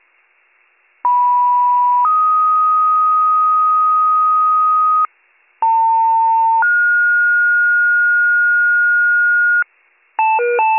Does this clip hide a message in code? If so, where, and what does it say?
Yes: two-tone page from 1.0 to 9.6 s, heard as a one-second tone, then a three-second tone, about 980 then 1250 Hz, about 900 then 1430 Hz, with noise.